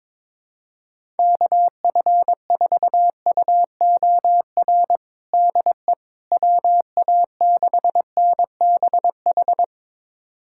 Morse code, heard as a keyed tone, about 21 wpm, 700 Hz. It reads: KF4UOR DE WA6NBH
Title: USA Callsign DE Callsign - 6 Characters - 22wpm